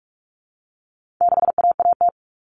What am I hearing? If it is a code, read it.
Morse code, heard as a keyed tone, about 44 wpm, 700 Hz. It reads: 6AAT
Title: Common Cut Numbers - 4-Digit 45wpm